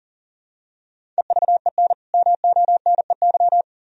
Morse code, heard as a keyed tone, about 40 wpm, 700 Hz. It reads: EVEN MONEY
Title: Sentences from Top 1000 Words - Review Part 4 - 40wpm